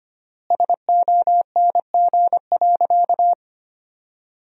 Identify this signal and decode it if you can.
Morse code, heard as a keyed tone, about 25 wpm, 700 Hz. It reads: SONG.
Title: 3x Word-Spacing - Sentences from Top 400 Words - 25wpm